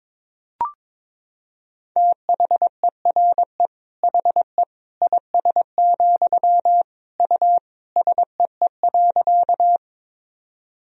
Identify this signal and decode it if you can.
Morse code, heard as a keyed tone, about 21 wpm, 700 Hz. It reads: THERE HE IS, U SEE.
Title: Sentences from Top 100 Words and Abbreviations 22wpm